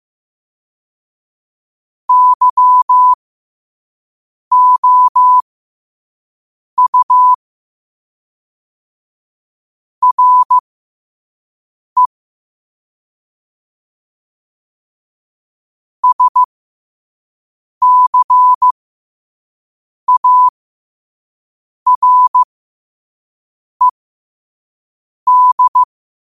Morse code, heard as a keyed tone, about 15 wpm, spaced as for 5 wpm, 1000 Hz. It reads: YOU RE SCARED
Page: Quotes for Tue, 12 Aug 2025 in Morse Code at 5 words per minute.